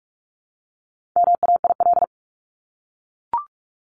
Morse code, heard as a keyed tone, about 44 wpm, 700 Hz. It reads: MAIL
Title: Top 1751-2000 Words 45wpm